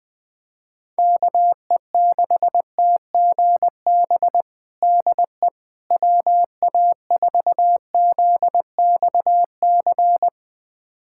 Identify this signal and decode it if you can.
Morse code, heard as a keyed tone, about 20 wpm, 700 Hz. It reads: KE6TGB DE WA4ZXC
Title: USA Callsign DE Callsign 20wpm